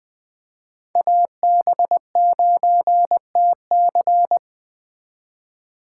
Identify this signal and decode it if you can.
Morse code, heard as a keyed tone, about 20 wpm, 700 Hz. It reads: AB9TC